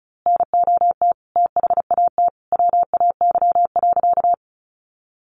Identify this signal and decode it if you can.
Morse code, heard as a keyed tone, about 34 wpm, 700 Hz. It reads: NOT THAT WAY.
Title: A Sound Of Thunder 35wpm